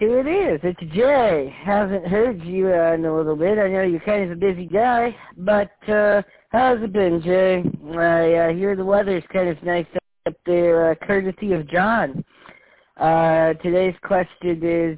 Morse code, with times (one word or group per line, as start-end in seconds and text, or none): none